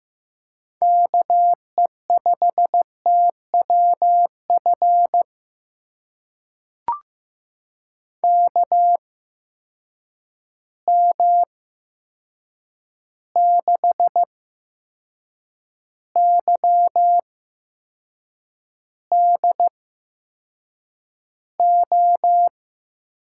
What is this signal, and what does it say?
Morse code, heard as a keyed tone, about 15 wpm, spaced as for 6 wpm, 700 Hz. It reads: KE5TWF KM6YDO
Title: USA Call Signs - 6 Characters - 8x Character Spacing - 15wpm